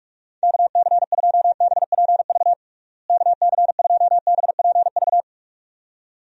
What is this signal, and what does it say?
Morse code, heard as a keyed tone, about 46 wpm, 700 Hz. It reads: KC2BPV KC2BPV
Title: USA Call Signs - 6 Characters - Repeated 2x - 45wpm